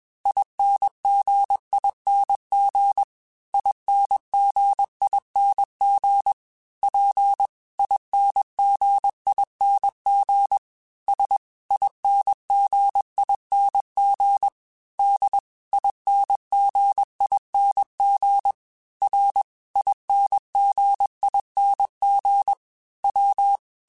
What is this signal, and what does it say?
Morse code, heard as a keyed tone, about 21 wpm, 800 Hz. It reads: INGING INGING P INGING S INGING D INGING R INGING W
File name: Special INGs / Special INGs-mp3s / Dble Ings sp @20